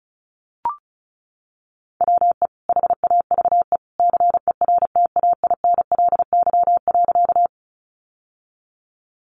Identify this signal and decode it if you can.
Morse code, heard as a keyed tone, about 34 wpm, 700 Hz. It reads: WE HAVE CERTAINLY.
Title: Sentences from Top 900 Words 35wpm